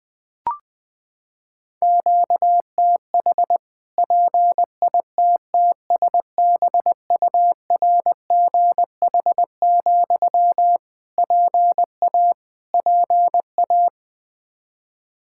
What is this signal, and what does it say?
Morse code, heard as a keyed tone, about 20 wpm, 700 Hz. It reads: QTH PITTSBURGH, PA PA
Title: Practice QSO - Exchange 1 - Beginning - 20wpm